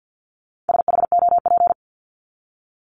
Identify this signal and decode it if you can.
Morse code, heard as a keyed tone, about 48 wpm, 700 Hz. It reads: SHOP
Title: Top 1501-1750 Words 50wpm